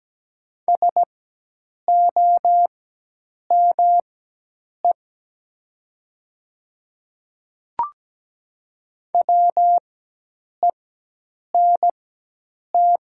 Morse code, heard as a keyed tone, about 17 wpm, spaced as for 8 wpm, 700 Hz. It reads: SOME WENT
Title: 4x Character Spacing - Top 200 Words - 17wpm